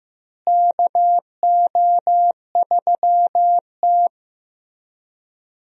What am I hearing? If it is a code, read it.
Morse code, heard as a keyed tone, about 15 wpm, 700 Hz. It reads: KO3T
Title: USA Call Signs - 4 Characters - Repeated 2x - 15wpm